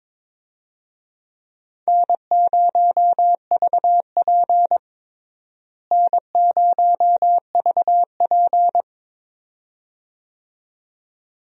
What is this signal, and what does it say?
Morse code, heard as a keyed tone, about 22 wpm, 700 Hz. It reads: N0VP N0VP
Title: USA Call Signs - 4 Characters - Repeated 2x - 22wpm